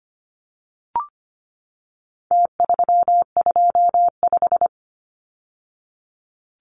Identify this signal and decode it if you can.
Morse code, heard as a keyed tone, about 25 wpm, 700 Hz. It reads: T325
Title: Numbers - Contest Style - 4-Digit 25wpm